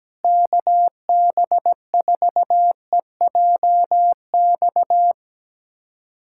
Morse code, heard as a keyed tone, about 17 wpm, 700 Hz. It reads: KB4EJX
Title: USA Callsign DE Callsign - 6 Characters - 17wpm